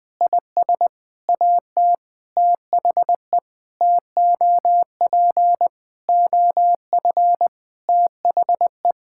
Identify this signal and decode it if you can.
Morse code, heard as a keyed tone, about 21 wpm, 700 Hz. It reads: IS AT THE TOP OF THE
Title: Sentences with 7-Word Phrases - Spoken Beginning - 20wpm